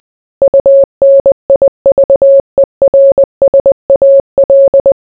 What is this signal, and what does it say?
Morse code, heard as a keyed tone, about 20 wpm, 550 Hz. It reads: UNIVERSAL